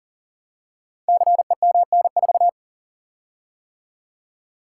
Morse code, heard as a keyed tone, about 40 wpm, 690 Hz. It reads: CEMN4
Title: Code Groups - 5 Letter:Numbers - 40wpm